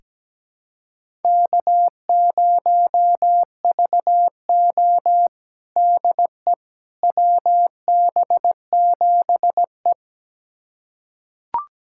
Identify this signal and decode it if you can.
Morse code, heard as a keyed tone, about 17 wpm, 700 Hz. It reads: K0VO DE WB7E
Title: USA Callsign DE Callsign - 4 Characters - 17wpm